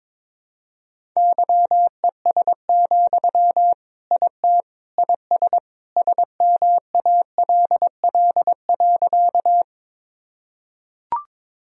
Morse code, heard as a keyed tone, about 22 wpm, 700 Hz. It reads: YES, IT IS SMALL.
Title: Sentences from Top 1000 Words - Review Part 1 - 22wpm